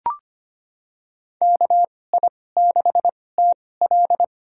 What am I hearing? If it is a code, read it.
Morse code, heard as a keyed tone, about 25 wpm, spaced as for 18 wpm, 700 Hz. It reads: KI6TL
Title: USA Call Signs - 5 Characters - 2x Character Spacing - 25wpm